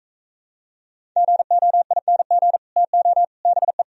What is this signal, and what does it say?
Morse code, heard as a keyed tone, about 45 wpm, 700 Hz. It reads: GOING TO BE